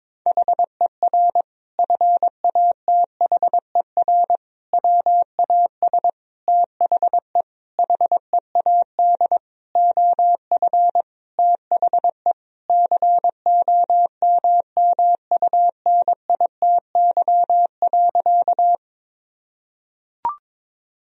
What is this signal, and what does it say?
Morse code, heard as a keyed tone, about 22 wpm, 700 Hz. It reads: HER FATHER WAS THE HEAD OF THE COMMUNITY.
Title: Sentences from Top 1000 Words - Review Part 4 - 22wpm